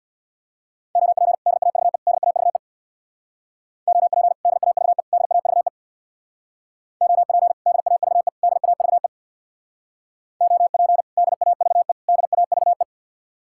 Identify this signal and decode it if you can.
Morse code, heard as a keyed tone, about 61 wpm, 700 Hz. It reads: OP DAVE DAVE OP DAVE DAVE OP DAVE DAVE OP DAVE DAVE
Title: Speed-Racing Common QSO Phrases Encore - 50wpm